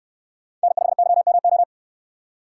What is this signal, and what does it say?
Morse code, heard as a keyed tone, about 68 wpm, 700 Hz. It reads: N5YMO